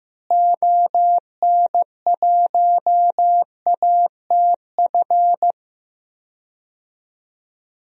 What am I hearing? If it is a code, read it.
Morse code, heard as a keyed tone, about 15 wpm, 700 Hz. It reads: ON1ATF
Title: DXCC-100 Call Signs - 6 Characters - Repeated 3x - 15wpm